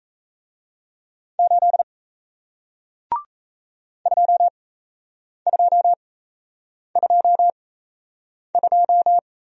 Morse code, heard as a keyed tone, about 37 wpm, spaced as for 8 wpm, 700 Hz. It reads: 8 2222